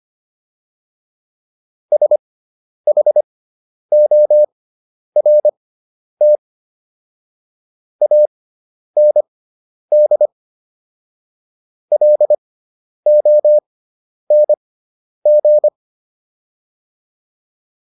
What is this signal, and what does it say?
Morse code, heard as a keyed tone, about 25 wpm, spaced as for 10 wpm, 600 Hz. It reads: SHORT AND LONG